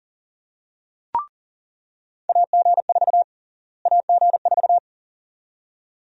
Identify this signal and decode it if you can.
Morse code, heard as a keyed tone, about 40 wpm, 700 Hz. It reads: AG4 AG4
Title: USA Call Sign Prefixes - Repeated 2x - 40wpm